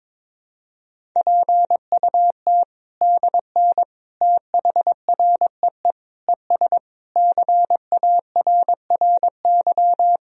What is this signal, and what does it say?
Morse code, heard as a keyed tone, about 22 wpm, 700 Hz. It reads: PUT DN THREE ES CARRY
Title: Sentences from Top 400 Words and Abbreviations 22wpm